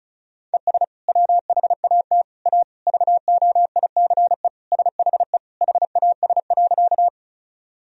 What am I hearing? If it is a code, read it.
Morse code, heard as a keyed tone, about 36 wpm, 700 Hz. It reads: ES WHAT A VOICE SHE HAS.
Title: Sentences from Top 500 Words and Abbreviations 35wpm